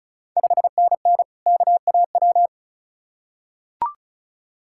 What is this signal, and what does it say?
Morse code, heard as a keyed tone, about 36 wpm, 700 Hz. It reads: 5NN KAW
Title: Ontario QSO Party Exchange 35wpm